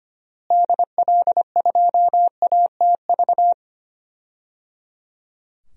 Morse code, heard as a keyed tone, about 25 wpm, 700 Hz. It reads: DL2ATV
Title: DXCC-100 Call Signs - 6 Characters - Repeated 3x - 25wpm